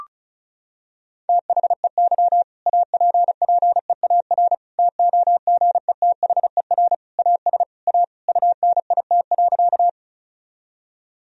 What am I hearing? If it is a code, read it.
Morse code, heard as a keyed tone, about 36 wpm, 700 Hz. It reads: THEY APPEAR TOGETHER AS A UNIT.
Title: Sentences from Top 1000 Words - Review Part 2 - 35wpm